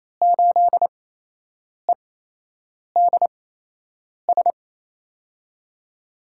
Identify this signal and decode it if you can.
Morse code, heard as a keyed tone, about 27 wpm, spaced as for 8 wpm, 700 Hz. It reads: 8EDS
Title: USA Call Signs - 5 Characters - 8x Character Spacing - 28wpm